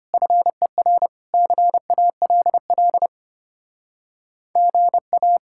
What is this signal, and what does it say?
Morse code, heard as a keyed tone, about 33 wpm, 700 Hz. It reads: FER CALL GA